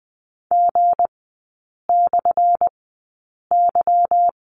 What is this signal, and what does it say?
Morse code, heard as a keyed tone, about 20 wpm, spaced as for 8 wpm, 700 Hz. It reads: G/Y
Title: 178 Lesson 29e - Slash - Three Characters 20wpm